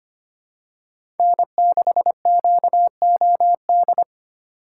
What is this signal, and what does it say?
Morse code, heard as a keyed tone, about 25 wpm, 700 Hz. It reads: N6QOD
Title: USA Call Signs - 5 Characters - 3x Character Spacing - 25wpm